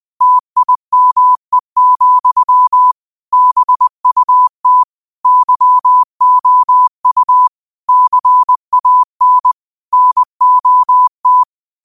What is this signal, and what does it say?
Morse code, heard as a keyed tone, about 20 wpm, 1000 Hz. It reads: TIME, BUT YOU CAN NOT
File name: qotdfiles / 20wpm